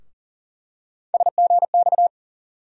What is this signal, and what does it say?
Morse code, heard as a keyed tone, about 40 wpm, 700 Hz. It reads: SGX
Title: USA Call Sign Postfixes - 3 Letters 40wpm